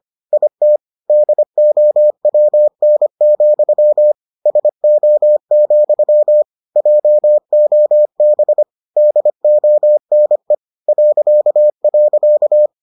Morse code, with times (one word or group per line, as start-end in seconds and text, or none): 0.3-0.8 IT
1.1-4.1 DOWN,
4.5-6.4 SO,
6.8-8.6 JOB
9.0-10.5 DONE
10.9-12.7 ..